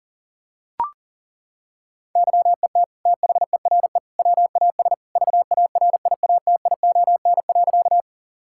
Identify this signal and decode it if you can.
Morse code, heard as a keyed tone, about 40 wpm, 700 Hz. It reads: YET THERE WAS VARIATION.
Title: Sentences from Top 1600 Words 40wpm